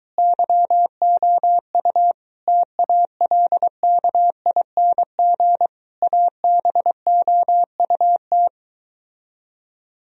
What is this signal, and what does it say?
Morse code, heard as a keyed tone, about 22 wpm, 700 Hz. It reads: YOU TALKING ABOUT